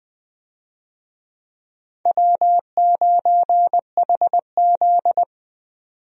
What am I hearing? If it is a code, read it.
Morse code, heard as a keyed tone, about 21 wpm, 700 Hz. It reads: W9HZ
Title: USA Call Signs - 4 Characters - 20wpm